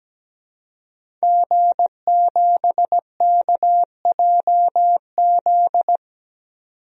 Morse code, heard as a keyed tone, about 17 wpm, 700 Hz. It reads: G7KJZ
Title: DXCC-100 Call Signs - 5 Characters - 17wpm